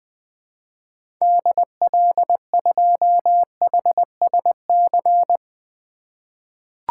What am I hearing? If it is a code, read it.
Morse code, heard as a keyed tone, about 21 wpm, 700 Hz. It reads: DL2HSC
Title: 2023 SOTA Activator Call Signs - 20wpm